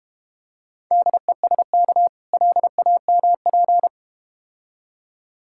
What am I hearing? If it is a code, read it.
Morse code, heard as a keyed tone, about 32 wpm, 700 Hz. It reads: DESK LAMP